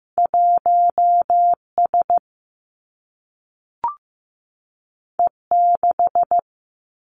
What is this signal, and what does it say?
Morse code, heard as a keyed tone, about 15 wpm, 700 Hz. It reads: 1 S E 6
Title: Two Challenging Letter-Number Combinations 15wpm